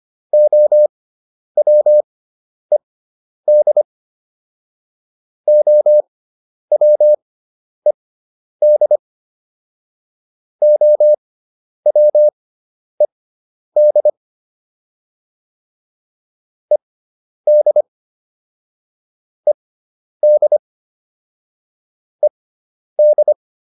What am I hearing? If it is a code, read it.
Morse code, heard as a keyed tone, about 25 wpm, spaced as for 10 wpm, 600 Hz. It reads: OWED OWED OWED ED ED ED